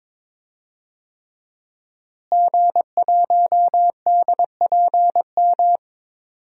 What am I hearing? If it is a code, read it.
Morse code, heard as a keyed tone, about 22 wpm, 700 Hz. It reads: G1DPM
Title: DXCC-100 Call Signs - 5 Characters - Repeated 3x - 22wpm